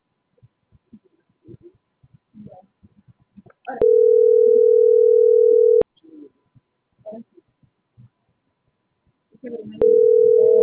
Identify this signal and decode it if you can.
Ringback tone (North America), 440 + 480 Hz.